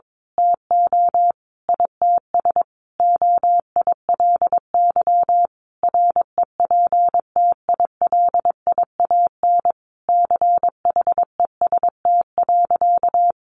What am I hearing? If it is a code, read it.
Morse code, heard as a keyed tone, about 22 wpm, 700 Hz. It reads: TO ITS OILY REPTILIAN CHEST.
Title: A Sound Of Thunder 22wpm